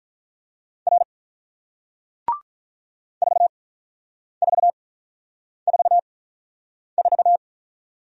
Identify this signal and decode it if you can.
Morse code, heard as a keyed tone, about 46 wpm, spaced as for 9 wpm, 700 Hz. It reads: R 4444